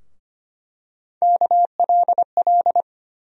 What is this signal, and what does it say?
Morse code, heard as a keyed tone, about 25 wpm, 700 Hz. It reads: KLL
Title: USA Call Sign Postfixes - 3 Letters 25wpm